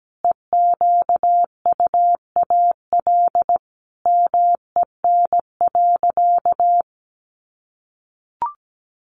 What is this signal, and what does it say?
Morse code, heard as a keyed tone, about 17 wpm, 700 Hz. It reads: EQUAL MEN.
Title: Sentences from Top 1400 Words 17wpm